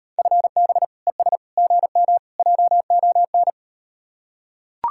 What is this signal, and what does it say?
Morse code, heard as a keyed tone, about 38 wpm, 700 Hz. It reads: FB ES GM JON